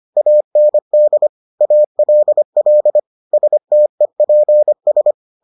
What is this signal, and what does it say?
Morse code, heard as a keyed tone, about 25 wpm, 600 Hz. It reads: AND ALL STEPS